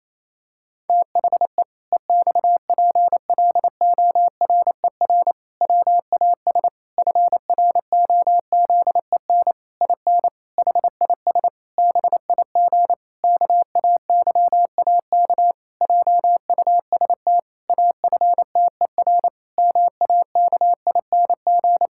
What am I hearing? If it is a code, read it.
Morse code, heard as a keyed tone, about 28 wpm, 700 Hz. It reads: THE EXPLORER WAS FROZEN IN HIS BIG KAYAK JUST AFTER MAKING
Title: Pangram Challenge v2 - 28wpm